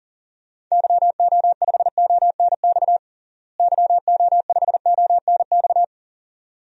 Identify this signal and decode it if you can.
Morse code, heard as a keyed tone, about 40 wpm, 700 Hz. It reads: YO5ONX YO5ONX
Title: DXCC-100 Call Signs - 6 Characters - Repeated 2x - 40wpm